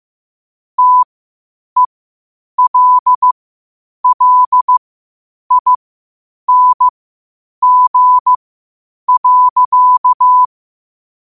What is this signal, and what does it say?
Morse code, heard as a keyed tone, about 15 wpm, spaced as for 8 wpm, 1000 Hz. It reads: TELLING.
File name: qotdfiles / 8wpm